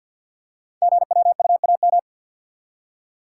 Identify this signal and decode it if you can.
Morse code, heard as a keyed tone, about 49 wpm, 700 Hz. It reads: GWUAM
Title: Code Groups - 5 Letters - 50wpm